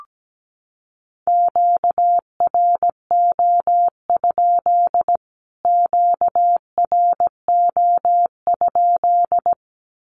Morse code, heard as a keyed tone, about 17 wpm, 700 Hz. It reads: QRO? QRO?
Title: Q-Codes Repeated 3x 17wpm